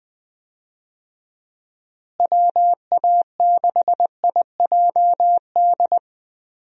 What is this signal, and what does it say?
Morse code, heard as a keyed tone, about 21 wpm, 700 Hz. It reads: WA6IJD
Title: USA Call Signs - 6 Characters - Repeated 3x - 20wpm